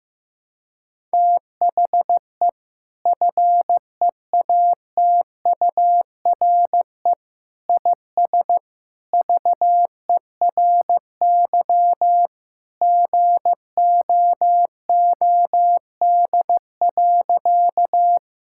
Morse code, heard as a keyed tone, about 15 wpm, 700 Hz. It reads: THE FEATURE IS VERY GOOD.
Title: Sentences from Top 1000 Words - Review Part 5 - 15wpm